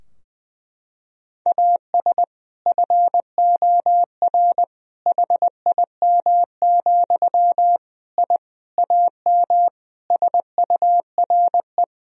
Morse code, heard as a keyed tone, about 20 wpm, 700 Hz. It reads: AS FOR HIM, I AM SURE
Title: Sentences from Top 1000 Words - Review Part 2 - 20wpm